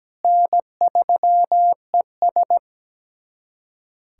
Morse code, heard as a keyed tone, about 17 wpm, 700 Hz. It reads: N3ES
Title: USA Callsign DE Callsign - 4 Characters - 17wpm